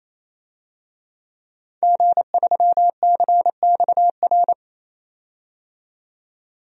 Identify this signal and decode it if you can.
Morse code, heard as a keyed tone, about 27 wpm, 700 Hz. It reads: G3CXR
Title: DXCC-100 Call Signs - 5 Characters - 28wpm